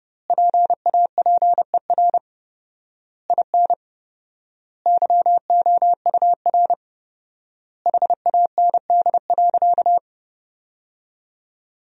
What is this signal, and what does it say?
Morse code, heard as a keyed tone, about 30 wpm, 700 Hz. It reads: PAPER IN YOUR HAND.
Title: 4x Word-Spacing - Sentences from Top 400 Words - 30wpm